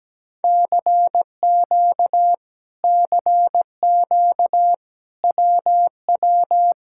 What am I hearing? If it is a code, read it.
Morse code, heard as a keyed tone, about 17 wpm, 700 Hz. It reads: CQ CQ WW
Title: Speed-Racing Common QSO Phrases - 17wpm